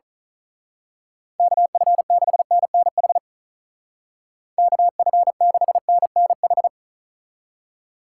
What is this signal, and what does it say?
Morse code, heard as a keyed tone, about 39 wpm, 700 Hz. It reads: KF6NNH KF6NNH
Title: Speed Racing - USA 2x3 Call Signs 35wpm